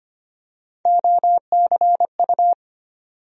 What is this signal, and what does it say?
Morse code, heard as a keyed tone, about 25 wpm, 700 Hz. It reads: OCU
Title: USA Call Sign Postfixes - 3 Letters 25wpm